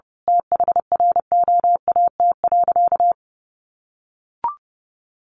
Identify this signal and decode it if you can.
Morse code, heard as a keyed tone, about 29 wpm, 700 Hz. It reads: THROAT.
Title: A Sound Of Thunder 30wpm